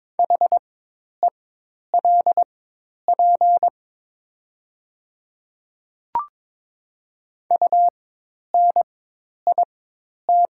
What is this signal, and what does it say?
Morse code, heard as a keyed tone, about 22 wpm, spaced as for 10 wpm, 700 Hz. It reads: HELP UNIT
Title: 4x Character Spacing - Top 500 Words - 22wpm